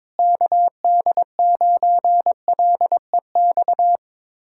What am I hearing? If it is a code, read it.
Morse code, heard as a keyed tone, about 21 wpm, 700 Hz. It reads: KD9LEX
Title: USA Callsign DE Callsign - 6 Characters - 22wpm